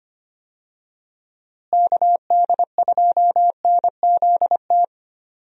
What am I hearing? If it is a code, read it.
Morse code, heard as a keyed tone, about 25 wpm, 700 Hz. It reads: KD2NZT
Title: USA Call Signs - 6 Characters - Repeated 2x - 25wpm